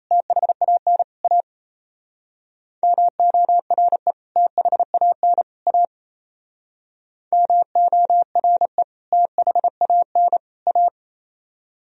Morse code, heard as a keyed tone, about 35 wpm, 700 Hz. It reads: THAN A MORE THAN A MORE THAN A